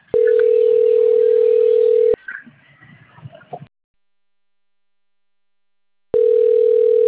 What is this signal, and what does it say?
Ringback tone (North America), 440 + 480 Hz.
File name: rg-1600-unknown- - - .56602